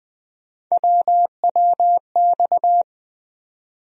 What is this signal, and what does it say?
Morse code, heard as a keyed tone, about 20 wpm, 700 Hz. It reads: WWX
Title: USA Call Sign Postfixes - 3 Letters 20wpm